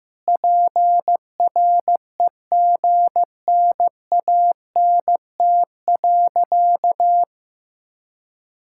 Morse code, heard as a keyed tone, about 15 wpm, 700 Hz. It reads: PREGNANT.
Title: Sentences from Top 1000 Words - Review Part 6 - 15wpm